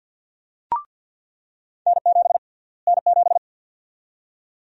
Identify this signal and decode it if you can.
Morse code, heard as a keyed tone, about 47 wpm, 700 Hz. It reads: N7 N7
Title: USA Call Sign Prefixes - Repeated 2x - 50wpm